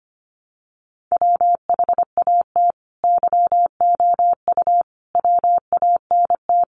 Morse code, heard as a keyed tone, about 25 wpm, 700 Hz. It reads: WHAT YOU WANT
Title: Sets of 5 Words Encore 25wpm